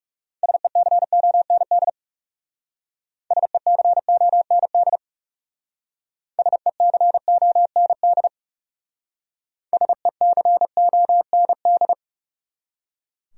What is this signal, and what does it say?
Morse code, heard as a keyed tone, about 35 wpm, 700 Hz. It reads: SECOND SECOND SECOND SECOND
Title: Speed-Racing Top 500 Words 30wpm